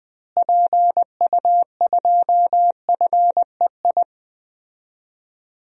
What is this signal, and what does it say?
Morse code, heard as a keyed tone, about 20 wpm, 700 Hz. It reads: PU2FEI
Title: DXCC-100 Call Signs - 6 Characters - Repeated 3x - 20wpm